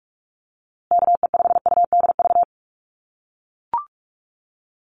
Morse code, heard as a keyed tone, about 44 wpm, 700 Hz. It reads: KE5UDV
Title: USA Call Signs Repeated 2x 45wpm